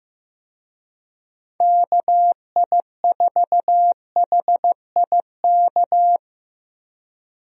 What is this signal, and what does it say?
Morse code, heard as a keyed tone, about 15 wpm, 700 Hz. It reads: KI4HIK
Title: USA Call Signs - 6 Characters - Repeated 3x - 15wpm